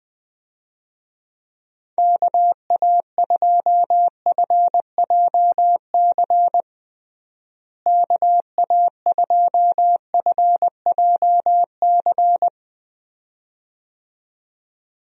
Morse code, heard as a keyed tone, about 21 wpm, 700 Hz. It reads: KA2FJC KA2FJC
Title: USA Call Signs - 6 Characters - Repeated 2x - 20wpm